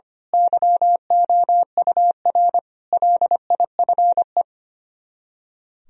Morse code, heard as a keyed tone, about 24 wpm, 700 Hz. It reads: YOUR LIFE
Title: Speed-Racing - Sets of 5 Words 25wpm